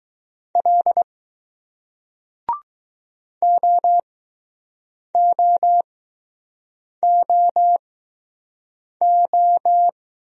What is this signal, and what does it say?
Morse code, heard as a keyed tone, about 20 wpm, spaced as for 6 wpm, 700 Hz. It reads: L OOOO